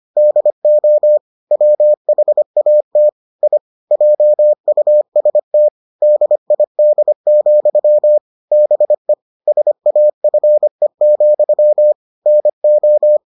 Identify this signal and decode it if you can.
Morse code, heard as a keyed tone, about 25 wpm, 600 Hz. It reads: DO WHAT I JUST DID, BE SAFE, NO